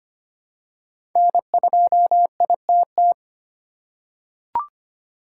Morse code, heard as a keyed tone, about 25 wpm, 700 Hz. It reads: N2ITT
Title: USA Call Signs - 5 Characters - 4x Character Spacing - 25wpm